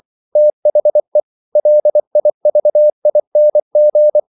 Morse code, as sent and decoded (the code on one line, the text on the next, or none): - .... / .-.. .. ...- .. -. --.
THE LIVING